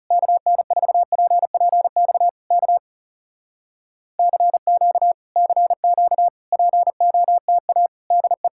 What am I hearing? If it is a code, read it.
Morse code, heard as a keyed tone, about 40 wpm, 700 Hz. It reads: KN4PPX K CQ CQ POTA DE